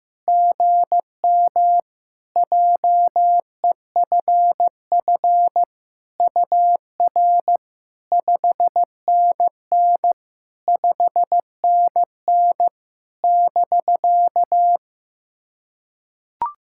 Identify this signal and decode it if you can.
Morse code, heard as a keyed tone, about 16 wpm, 700 Hz. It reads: GM JEFF UR 5NN 5NN <BK>
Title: SOTA QSO Practice 15wpm